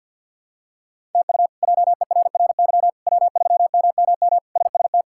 Morse code, heard as a keyed tone, about 50 wpm, 700 Hz. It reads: TU JERRY W3MMM SST
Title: K1USN SST QSO Practice - 50wpm